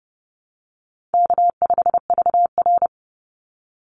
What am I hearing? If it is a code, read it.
Morse code, heard as a keyed tone, about 30 wpm, 700 Hz. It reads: K5VR
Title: Easy Call Signs Repeated 3x - 30wpm